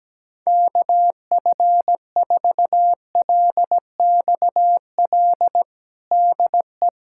Morse code, heard as a keyed tone, about 17 wpm, 700 Hz. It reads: KF4LXL DE
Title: USA Callsign DE Callsign - 6 Characters - 17wpm